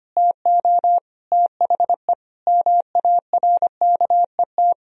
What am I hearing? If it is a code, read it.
Morse code, heard as a keyed tone, about 25 wpm, 700 Hz. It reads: TO THE MARKET
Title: Sentences from Top 1000 Words - Review Part 3 - 25wpm